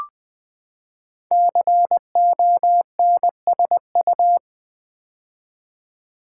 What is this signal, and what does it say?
Morse code, heard as a keyed tone, about 20 wpm, 700 Hz. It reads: CONSU
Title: Build A Word - 5 Letters At A Time - 12-Letter Words - 20wpm